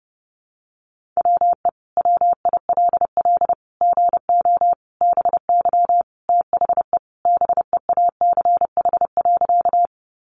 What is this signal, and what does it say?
Morse code, heard as a keyed tone, about 30 wpm, 700 Hz. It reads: WE WILL GO BY THE BEACH.
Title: Sentences from Top 1500 Words 30wpm